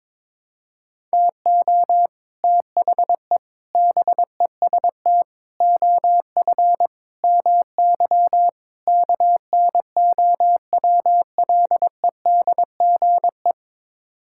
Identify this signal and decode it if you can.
Morse code, heard as a keyed tone, about 22 wpm, 700 Hz. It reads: TO THE BEST OF MY KNOWLEDGE
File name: practice / 22wpm / Sets of 6 Words - 22wpm